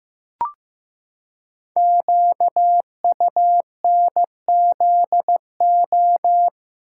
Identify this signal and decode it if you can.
Morse code, heard as a keyed tone, about 15 wpm, 700 Hz. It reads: QUNZO
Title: Code Groups - 5 Letters - 15wpm